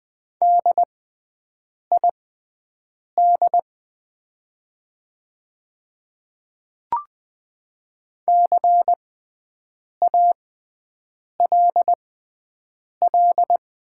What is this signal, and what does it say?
Morse code, heard as a keyed tone, about 20 wpm, spaced as for 7 wpm, 700 Hz. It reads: DID CALL
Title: 6x Character Spacing - Top 100 Words - 20wpm